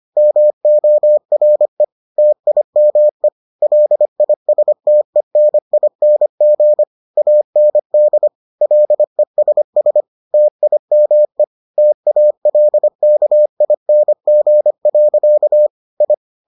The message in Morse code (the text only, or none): MORE TIME LISTENING AND LESS TIME TALKING. I